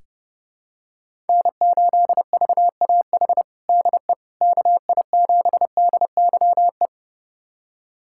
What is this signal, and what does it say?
Morse code, heard as a keyed tone, about 32 wpm, 700 Hz. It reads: N8VAH DE KI7DYE
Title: USA Callsign DE Callsign 30wpm